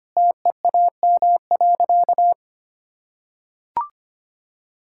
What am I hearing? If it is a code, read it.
Morse code, heard as a keyed tone, about 25 wpm, 700 Hz. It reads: TEAM.
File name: practice / 25wpm / Sentences with 2-Word Phrases - 25wpm